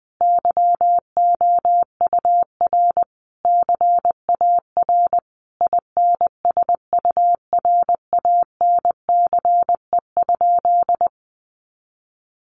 Morse code, heard as a keyed tone, about 20 wpm, 700 Hz. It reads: YOUR CAR INSURANCE?
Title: Sentences from Top 1100 Words 20wpm